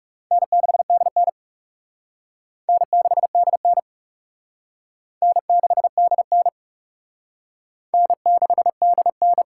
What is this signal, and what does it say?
Morse code, heard as a keyed tone, about 40 wpm, 700 Hz. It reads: N6DN N6DN N6DN N6DN